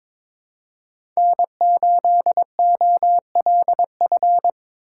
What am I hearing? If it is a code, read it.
Morse code, heard as a keyed tone, about 22 wpm, 700 Hz. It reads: N8OLF
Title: USA Call Signs - 5 Characters - 4x Character Spacing - 22wpm